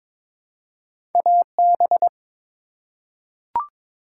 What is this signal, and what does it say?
Morse code, heard as a keyed tone, about 22 wpm, 700 Hz. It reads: AB
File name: ARRL Sections v2 - 22wpm